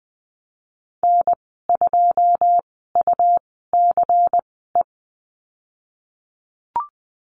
Morse code, heard as a keyed tone, about 20 wpm, spaced as for 15 wpm, 700 Hz. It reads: N2UCE
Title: 122 Lesson 20g - Number 3 - Call Signs 20wpm